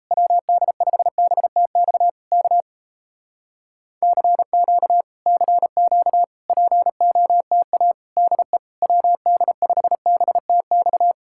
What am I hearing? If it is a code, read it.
Morse code, heard as a keyed tone, about 37 wpm, 700 Hz. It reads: WD5BTX K CQ CQ POTA DE WD5BTX